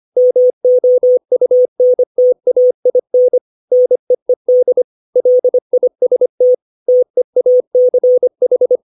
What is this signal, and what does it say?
Morse code, heard as a keyed tone, about 25 wpm, 500 Hz. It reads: MOUNTAIN NEED LIST TEACH
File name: 632 words / 632 Words 25WPM